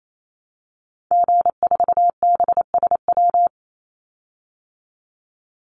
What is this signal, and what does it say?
Morse code, heard as a keyed tone, about 28 wpm, 700 Hz. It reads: G4BSW
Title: Call Signs Repeated 2x - 28wpm